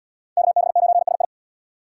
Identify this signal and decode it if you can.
Morse code, heard as a keyed tone, about 77 wpm, 700 Hz. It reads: KB9SI